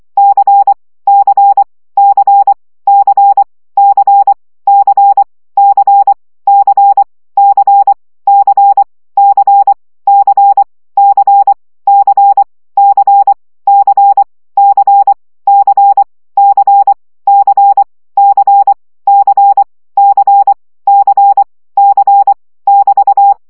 Morse code, heard as a keyed tone, about 24 wpm, 800 Hz. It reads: C C C C C C C C C C C C C C C C C C C C C C C C C <BT>
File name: CW gyakorlas c 20 baud